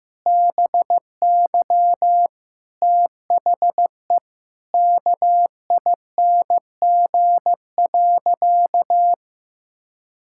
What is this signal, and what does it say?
Morse code, heard as a keyed tone, about 15 wpm, 700 Hz. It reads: BY THE KING.
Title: Sentences from Top 1000 Words - Review Part 5 - 15wpm